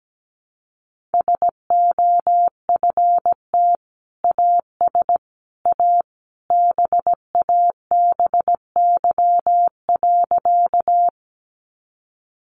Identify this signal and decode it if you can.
Morse code, heard as a keyed tone, about 17 wpm, 700 Hz. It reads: SOFT AS A BABY.
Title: Sentences from Top 1500 Words 17wpm